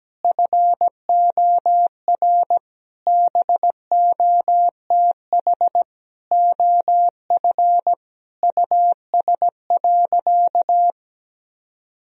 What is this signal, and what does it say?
Morse code, heard as a keyed tone, about 17 wpm, 700 Hz. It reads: FOR BOTH OF US.
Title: Sentences from Top 1000 Words - Review Part 1 - 17wpm